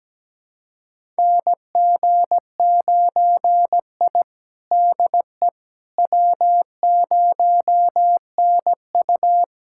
Morse code, heard as a keyed tone, about 17 wpm, 700 Hz. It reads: NG9I DE W0NU
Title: USA Callsign DE Callsign - 4 Characters - 17wpm